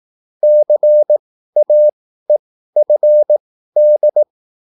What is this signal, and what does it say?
Morse code, heard as a keyed tone, about 18 wpm, spaced as for 13 wpm, 600 Hz. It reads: CAEFD